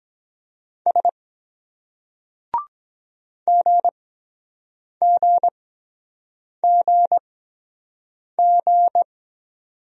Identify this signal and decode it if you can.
Morse code, heard as a keyed tone, about 23 wpm, spaced as for 7 wpm, 700 Hz. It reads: S GGGG